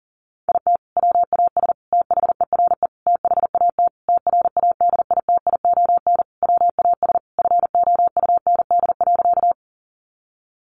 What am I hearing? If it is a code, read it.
Morse code, heard as a keyed tone, about 37 wpm, 700 Hz. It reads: IT WAS THERE THAT TRADITION WAS FOUND.
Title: Sentences from Top 1500 Words 40wpm